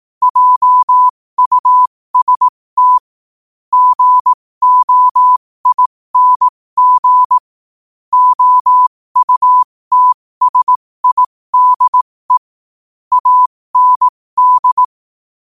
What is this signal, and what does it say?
Morse code, heard as a keyed tone, about 18 wpm, spaced as for 15 wpm, 1000 Hz. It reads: JUST GOING OUTSIDE AND
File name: qotdfiles / 15wpm